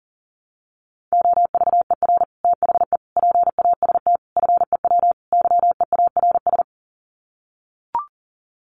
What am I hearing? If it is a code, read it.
Morse code, heard as a keyed tone, about 40 wpm, 700 Hz. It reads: OVER THE PAST FEW YEARS
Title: Sets of 5 Words Encore 40wpm